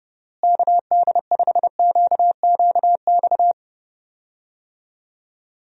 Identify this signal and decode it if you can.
Morse code, heard as a keyed tone, about 30 wpm, 700 Hz. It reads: KD5QQX
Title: USA Call Signs - 6 Characters - Repeated 2x - 30wpm